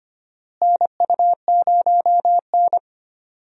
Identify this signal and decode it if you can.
Morse code, heard as a keyed tone, about 25 wpm, 700 Hz. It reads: NU0N
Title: USA Call Signs - 4 Characters - Repeated 2x - 25wpm